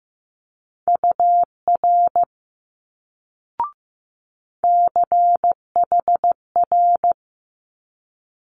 Morse code, heard as a keyed tone, about 15 wpm, 700 Hz. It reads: UR CHR
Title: Multi-letter Phonemes Quick Fire 15wpm